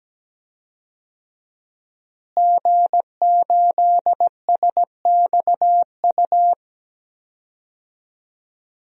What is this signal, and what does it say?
Morse code, heard as a keyed tone, about 17 wpm, 700 Hz. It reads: G8SXU
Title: DXCC-100 Call Signs - 5 Characters - Repeated 2x - 17wpm